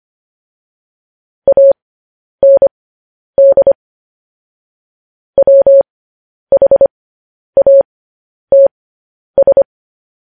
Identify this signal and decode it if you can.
Morse code, heard as a keyed tone, about 25 wpm, spaced as for 10 wpm, 550 Hz. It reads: AND WHATS
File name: podcast / News headlines-1x-25x10